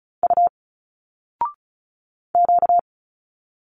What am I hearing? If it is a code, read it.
Morse code, heard as a keyed tone, about 36 wpm, 700 Hz. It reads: U Q
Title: Single Letters 35wpm